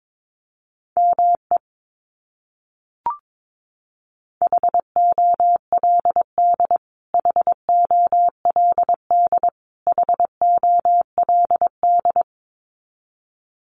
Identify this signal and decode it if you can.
Morse code, heard as a keyed tone, about 22 wpm, 700 Hz. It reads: ME HOLD HOLD HOLD
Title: Top 500 Words Repeated 22wpm